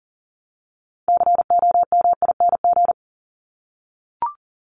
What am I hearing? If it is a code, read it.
Morse code, heard as a keyed tone, about 40 wpm, 700 Hz. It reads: COMING
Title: ING Suffix - Top 100 Words with Base Words 40wpm